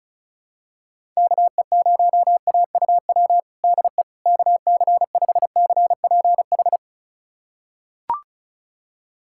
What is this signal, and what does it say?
Morse code, heard as a keyed tone, about 36 wpm, 690 Hz. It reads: KE0AUW DE KC5CPH
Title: USA Callsign DE Callsign - 6 Characters - 35wpm